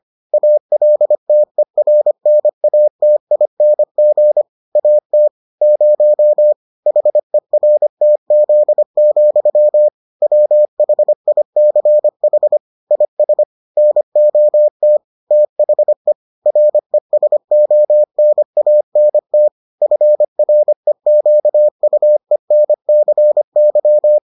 Morse code, heard as a keyed tone, about 25 wpm, 600 Hz. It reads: ALTERNATING AT 0 HERTZ, WHICH IS NOT THE RESONANT FREQUENCY